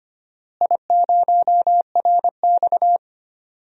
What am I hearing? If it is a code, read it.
Morse code, heard as a keyed tone, about 25 wpm, 700 Hz. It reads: I0RX
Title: DXCC-100 Call Signs - 4 Characters - Repeated 2x - 25wpm